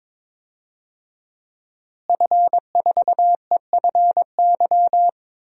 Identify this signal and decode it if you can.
Morse code, heard as a keyed tone, about 22 wpm, 700 Hz. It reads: F4EFY
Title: DXCC-100 Call Signs - 5 Characters - Repeated 3x - 22wpm